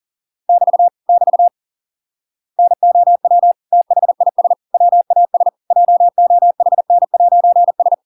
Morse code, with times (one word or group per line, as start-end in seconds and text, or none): 0.5-0.9 <BT>
1.1-1.5 <BT>
2.6-3.5 NOW
3.7-4.5 THIS
4.7-5.5 WAS
5.7-7.9 JOHN'S